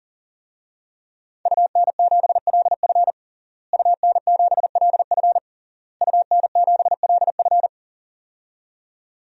Morse code, heard as a keyed tone, about 40 wpm, 700 Hz. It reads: UN7LF UN7LF UN7LF
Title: DXCC-100 Call Signs - 5 Characters - Repeated 3x - 40wpm